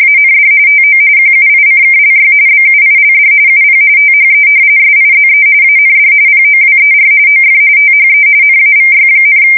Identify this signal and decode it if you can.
RTTY, about 45.45 baud. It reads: AND TO QUICKLY BUILD UP THE SIZE AND POWER OF THE NAVY,
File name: fsk170 / battleshipnj